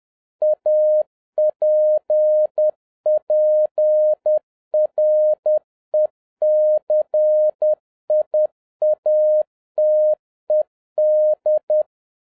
Morse code, heard as a keyed tone, about 10 wpm, 610 Hz. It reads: APPRECIATED